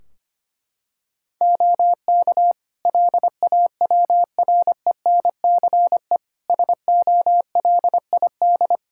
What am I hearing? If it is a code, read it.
Morse code, heard as a keyed tone, about 25 wpm, 700 Hz. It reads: OK LAWRENCE SOLID
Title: Practice QSO - Exchange 3 - More Info - 25wpm